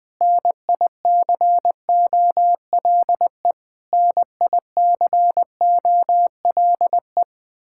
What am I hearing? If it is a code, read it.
Morse code, heard as a keyed tone, about 20 wpm, 700 Hz. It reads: NICOLE NICOLE
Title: Practice QSO - Exchange 1 - Beginning - 20wpm